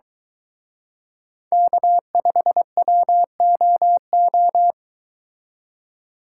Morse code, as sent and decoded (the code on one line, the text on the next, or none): -.- ..... .-- --- ---
K5WOO